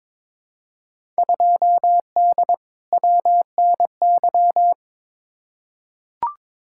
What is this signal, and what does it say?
Morse code, heard as a keyed tone, about 22 wpm, 700 Hz. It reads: 2D WNY
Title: Field Day Exchanges - 22wpm